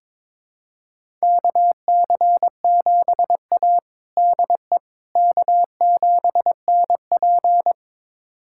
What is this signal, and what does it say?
Morse code, heard as a keyed tone, about 21 wpm, 700 Hz. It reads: KC7A DE K7NP
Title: USA Callsign DE Callsign - 4 Characters - 22wpm